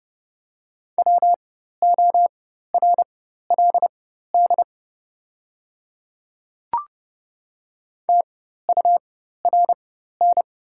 Morse code, heard as a keyed tone, about 31 wpm, spaced as for 14 wpm, 690 Hz. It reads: WORLD TURN
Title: 4x Character Spacing - Top 300 Words - 30wpm